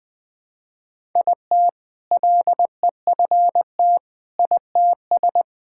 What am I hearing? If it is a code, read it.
Morse code, heard as a keyed tone, about 20 wpm, 700 Hz. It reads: IT LEFT ITS
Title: All-Age Appropriate Jokes - 20wpm